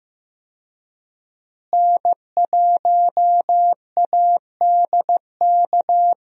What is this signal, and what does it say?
Morse code, heard as a keyed tone, about 15 wpm, 700 Hz. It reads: N1ADK
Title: USA Call Signs - 5 Characters - Repeated 3x - 15wpm